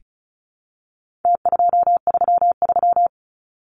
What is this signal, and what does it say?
Morse code, heard as a keyed tone, about 34 wpm, 700 Hz. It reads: T233
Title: Numbers - Contest Style - 4-Digit 35wpm